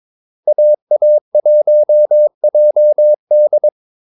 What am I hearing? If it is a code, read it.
Morse code, heard as a keyed tone, about 22 wpm, 600 Hz. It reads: AA1JD
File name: Callsigns-22wpm